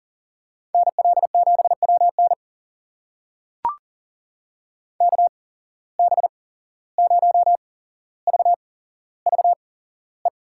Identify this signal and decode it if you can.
Morse code, heard as a keyed tone, about 40 wpm, spaced as for 16 wpm, 700 Hz. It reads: NL7WN KB0VVE
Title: USA Call Signs - 8x Character Spacing - 40wpm